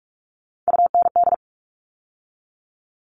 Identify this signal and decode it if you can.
Morse code, heard as a keyed tone, about 44 wpm, 700 Hz. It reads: UND
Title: All Cut Numbers - 3-Digit 45wpm